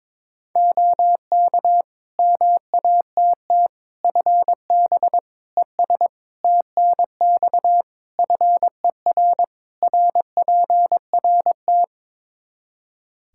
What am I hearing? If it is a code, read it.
Morse code, heard as a keyed tone, about 22 wpm, 700 Hz. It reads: OK MATT FB ES TNX FER RPRT
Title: Practice QSO - Exchange 2 - Info - 22wpm